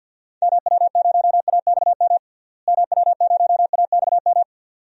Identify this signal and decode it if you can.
Morse code, heard as a keyed tone, about 51 wpm, 700 Hz. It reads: MW0AXM MW0AXM
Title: DXCC-100 Call Signs - 6 Characters - Repeated 2x - 50wpm